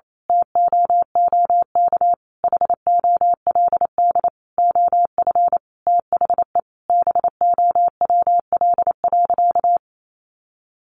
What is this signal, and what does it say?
Morse code, heard as a keyed tone, about 28 wpm, 700 Hz. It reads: TOOK HOLD OF THE BOWL.
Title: Sentences from Top 1200 Words 28wpm